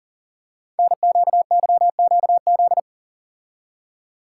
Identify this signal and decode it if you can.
Morse code, heard as a keyed tone, about 44 wpm, 700 Hz. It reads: NQYQZ
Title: Code Groups - 5 Letters - 40wpm